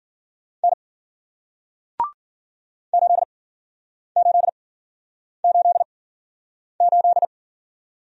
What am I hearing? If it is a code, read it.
Morse code, heard as a keyed tone, about 53 wpm, spaced as for 9 wpm, 700 Hz. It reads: N 8888